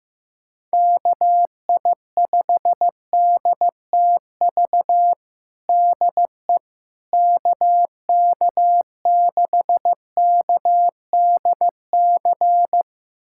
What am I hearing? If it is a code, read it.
Morse code, heard as a keyed tone, about 15 wpm, 700 Hz. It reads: KI5DTV DE KK6KDC
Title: USA Callsign DE Callsign - 6 Characters - 15wpm